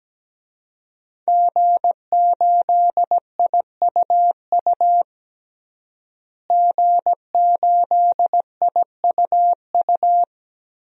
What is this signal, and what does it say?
Morse code, heard as a keyed tone, about 17 wpm, 700 Hz. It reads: G8IUU G8IUU
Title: DXCC-100 Call Signs - 5 Characters - Repeated 2x - 17wpm